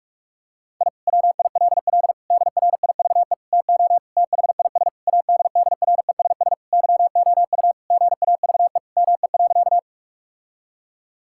Morse code, heard as a keyed tone, about 41 wpm, 700 Hz. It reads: I WILL DRIVE TO THIS ADDRESS YOU GAVE ME.
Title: Sentences from Top 1000 Words - Review Part 6 - 45wpm